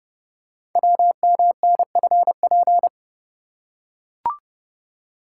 Morse code, heard as a keyed tone, about 30 wpm, 690 Hz. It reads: WMNFP
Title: Code Groups - 5 Letters - 30wpm